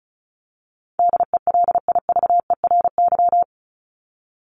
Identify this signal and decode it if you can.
Morse code, heard as a keyed tone, about 34 wpm, 700 Hz. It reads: DELIVERY
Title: Top 1251-1500 Words 35wpm